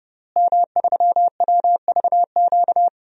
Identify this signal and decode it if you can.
Morse code, heard as a keyed tone, about 32 wpm, 700 Hz. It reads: M3WVQ
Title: DXCC-100 Call Signs - 5 Characters - Repeated 2x - 30wpm